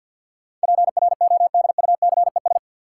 Morse code, heard as a keyed tone, about 52 wpm, 700 Hz. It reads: PRODUCES
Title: Sentences from Top 1000 Words - Review Part 6 - New Word Drill - 50wpm